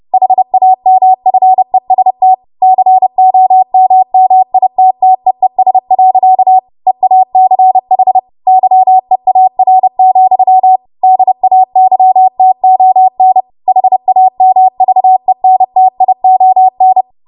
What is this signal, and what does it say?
Morse code, heard as a keyed tone, about 30 wpm, 750 Hz. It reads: HAMFEST COMMITTEES. EACH YEAR, DAYTON HAMVENTION
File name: Morse / 30WPM